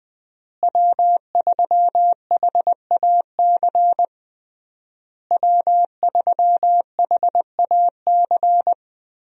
Morse code, heard as a keyed tone, about 20 wpm, 700 Hz. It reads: W3HAC W3HAC
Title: USA Call Signs - 5 Characters - Repeated 3x - 20wpm